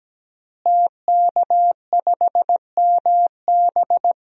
Morse code, heard as a keyed tone, about 17 wpm, 700 Hz. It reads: TK5MB
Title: DXCC-100 Call Signs - 5 Characters - Repeated 3x - 17wpm